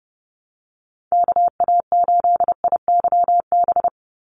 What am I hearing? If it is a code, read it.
Morse code, heard as a keyed tone, about 30 wpm, 700 Hz. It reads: KA8IYB
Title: USA Call Signs 30wpm